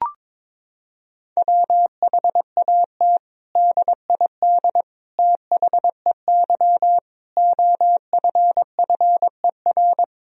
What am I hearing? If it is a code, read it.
Morse code, heard as a keyed tone, about 22 wpm, 700 Hz. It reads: WHAT DID THEY OFFER
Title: Sentences from Top 1000 Words - Review Part 4 - 22wpm